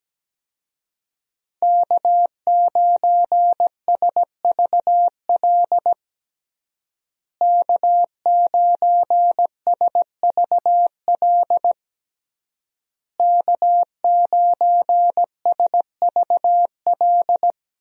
Morse code, heard as a keyed tone, about 17 wpm, 700 Hz. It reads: K9SVL K9SVL K9SVL
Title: USA Call Signs - 5 Characters - Repeated 3x - 17wpm